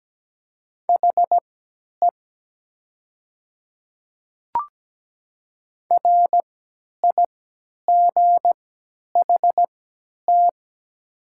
Morse code, heard as a keyed tone, about 17 wpm, spaced as for 10 wpm, 700 Hz. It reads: HE RIGHT
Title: 3x Character Spacing - Top 300 Words - 17wpm